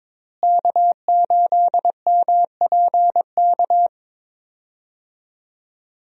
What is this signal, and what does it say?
Morse code, heard as a keyed tone, about 22 wpm, 700 Hz. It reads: K8MPK
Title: USA Call Signs - 5 Characters - Repeated 2x - 22wpm